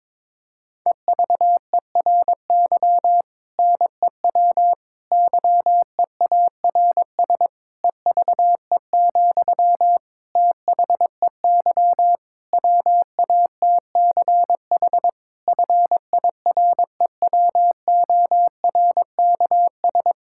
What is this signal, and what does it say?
Morse code, heard as a keyed tone, about 22 wpm, 700 Hz. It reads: EVERY NEW YEARS EVE, THEY WATCH FIREWORKS
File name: Difficult Sentences with Hints - 22wpm